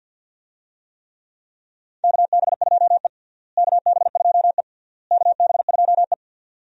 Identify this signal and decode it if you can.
Morse code, heard as a keyed tone, about 47 wpm, 700 Hz. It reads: KB2E KB2E KB2E
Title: USA Call Signs - 4 Characters - Repeated 3x - 50wpm